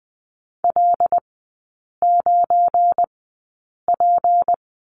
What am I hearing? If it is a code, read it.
Morse code, heard as a keyed tone, about 20 wpm, spaced as for 8 wpm, 700 Hz. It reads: L9P
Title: 172 Lesson 28e - Number 9 - Three Characters 20wpm